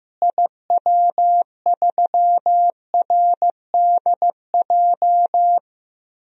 Morse code, heard as a keyed tone, about 15 wpm, 700 Hz. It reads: IW3RDJ
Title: DXCC-100 Call Signs - 6 Characters - Repeated 2x - 15wpm